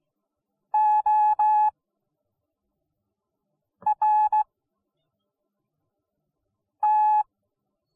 Morse code, heard as a keyed tone, about 17 wpm, spaced as for 4 wpm, 850 Hz. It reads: ORT